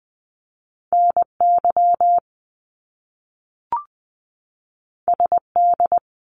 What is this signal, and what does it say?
Morse code, heard as a keyed tone, about 20 wpm, 700 Hz. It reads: NY SD
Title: US States 20wpm